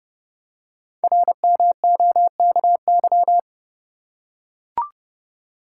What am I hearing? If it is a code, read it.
Morse code, heard as a keyed tone, about 30 wpm, 700 Hz. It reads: RMOKY
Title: Code Groups - 5 Letters - 30wpm